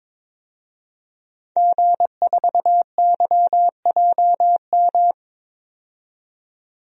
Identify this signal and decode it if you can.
Morse code, heard as a keyed tone, about 22 wpm, 700 Hz. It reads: G4YJM
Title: DXCC-100 Call Signs - 5 Characters - 22wpm